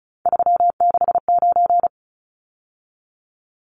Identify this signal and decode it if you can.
Morse code, heard as a keyed tone, about 34 wpm, 700 Hz. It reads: 369
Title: Numbers - 3-Digit 35wpm